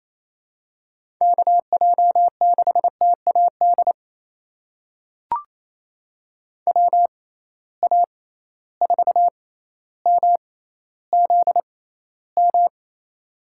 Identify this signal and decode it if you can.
Morse code, heard as a keyed tone, about 28 wpm, spaced as for 14 wpm, 700 Hz. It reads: KJ6TAD WA4MZM
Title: USA Call Signs - 6 Characters - 6x Character Spacing - 28wpm